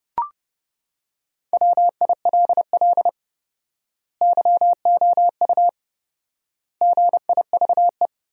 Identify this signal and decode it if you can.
Morse code, heard as a keyed tone, about 30 wpm, 700 Hz. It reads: WILL YOU GIVE
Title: 4x Word-Spacing - Sentences from Top 200 Words - 30wpm